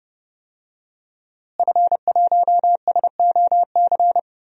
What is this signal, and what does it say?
Morse code, heard as a keyed tone, about 30 wpm, 700 Hz. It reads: F1SOC
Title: DXCC-100 Call Signs - 5 Characters - Repeated 2x - 30wpm